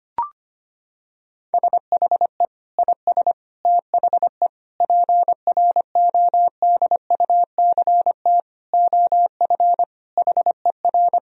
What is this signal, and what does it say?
Morse code, heard as a keyed tone, about 25 wpm, 700 Hz. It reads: SHE IS THE PRODUCT OF HER
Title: Sentences from Top 1000 Words - Review Part 2 - 25wpm